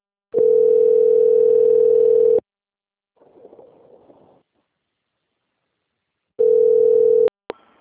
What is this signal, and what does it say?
Ringback tone (North America), 440 + 480 Hz.
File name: rg-1600-unknown- - - .50712